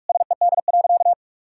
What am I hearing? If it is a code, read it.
Morse code, heard as a keyed tone, about 46 wpm, 700 Hz. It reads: SED.
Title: Sentences from Top 100 Words and Abbreviations 45wpm